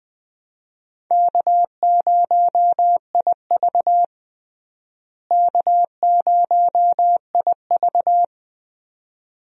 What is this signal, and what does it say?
Morse code, heard as a keyed tone, about 21 wpm, 700 Hz. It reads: K0IV K0IV
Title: USA Call Signs - 4 Characters - Repeated 2x - 20wpm